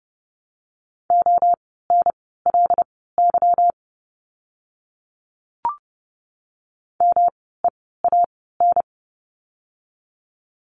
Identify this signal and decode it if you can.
Morse code, heard as a keyed tone, about 30 wpm, spaced as for 17 wpm, 700 Hz. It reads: ONLY MEAN
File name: course / 30:10wpm / 230 Lesson 37f - Letter x - Words 30wpm